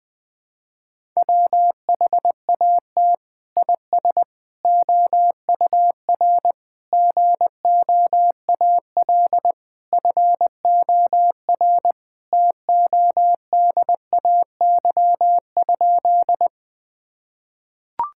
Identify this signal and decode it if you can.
Morse code, heard as a keyed tone, about 21 wpm, 700 Hz. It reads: WHAT IS OUR GOAL FOR TODAY?
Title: Sentences from Top 1000 Words - Review Part 5 - 20wpm